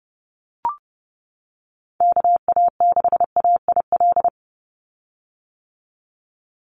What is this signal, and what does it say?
Morse code, heard as a keyed tone, about 30 wpm, 700 Hz. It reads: KA6AIL
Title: Call Signs - 30wpm